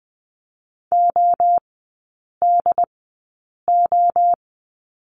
Morse code, heard as a keyed tone, about 20 wpm, spaced as for 8 wpm, 700 Hz. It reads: ODO
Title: 074 Lesson 13e - Letter L - Three Characters 20wpm